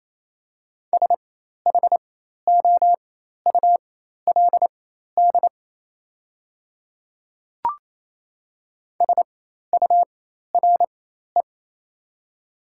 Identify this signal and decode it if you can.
Morse code, heard as a keyed tone, about 27 wpm, spaced as for 13 wpm, 700 Hz. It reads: SHOULD SURE
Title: 4x Character Spacing - Top 300 Words - 28wpm